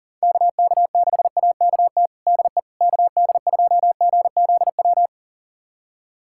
Morse code, heard as a keyed tone, about 40 wpm, 700 Hz. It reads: KK6AKT DE KD2GZW
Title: USA Callsign DE Callsign - 6 Characters - 40wpm